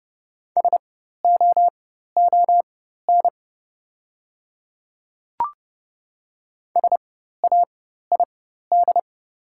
Morse code, heard as a keyed tone, about 30 wpm, spaced as for 14 wpm, 690 Hz. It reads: SOON SAID
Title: 4x Character Spacing - Top 500 Words - 30wpm